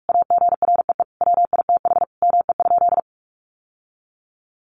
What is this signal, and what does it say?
Morse code, heard as a keyed tone, about 44 wpm, 700 Hz. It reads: AGREE WITH ME?
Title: Sentences from Top 1100 Words 45wpm